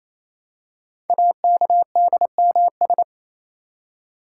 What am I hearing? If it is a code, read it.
Morse code, heard as a keyed tone, about 27 wpm, 700 Hz. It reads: AKDMS
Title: Code Groups - 5 Letters - 28wpm